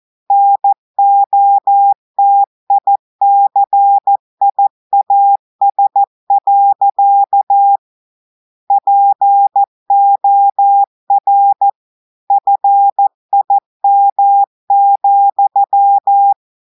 Morse code, heard as a keyed tone, about 14 wpm, 800 Hz. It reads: NOTICIAS. POR FIM,
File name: audios / portugais / portugais texte en clair 2 vitesse 14 mots minutes